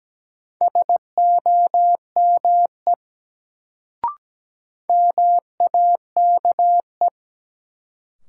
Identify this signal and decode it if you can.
Morse code, heard as a keyed tone, about 17 wpm, 700 Hz. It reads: SOME MAKE
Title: Sets of 2 Words - New Word Drill - 17wpm